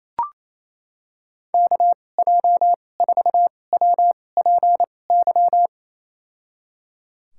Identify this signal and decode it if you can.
Morse code, heard as a keyed tone, about 27 wpm, spaced as for 20 wpm, 700 Hz. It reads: KJ4WPY
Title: USA Call Signs - 6 Characters - 2x Character Spacing - 28wpm